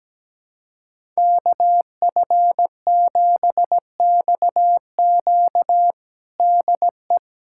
Morse code, heard as a keyed tone, about 17 wpm, 700 Hz. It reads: KF7XQ DE
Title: USA Callsign DE Callsign - 5 Characters - 17wpm